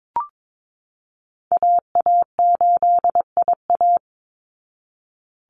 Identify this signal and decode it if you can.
Morse code, heard as a keyed tone, about 22 wpm, 700 Hz. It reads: AA8IA
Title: Call Signs Repeated 3x - 22wpm